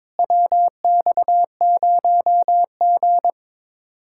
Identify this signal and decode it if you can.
Morse code, heard as a keyed tone, about 23 wpm, 700 Hz. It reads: WX0G
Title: Speed-Racing Common QSO Phrases - 22wpm